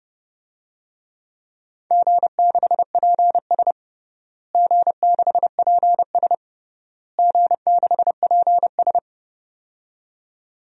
Morse code, heard as a keyed tone, about 30 wpm, 700 Hz. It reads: G6PS G6PS G6PS
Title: DXCC-100 Call Signs - 4 Characters - Repeated 3x - 30wpm